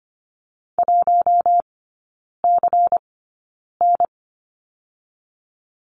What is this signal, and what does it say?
Morse code, heard as a keyed tone, about 25 wpm, spaced as for 9 wpm, 700 Hz. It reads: 1CN
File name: Lesson 17e - Letter u - Three Characters 25wpm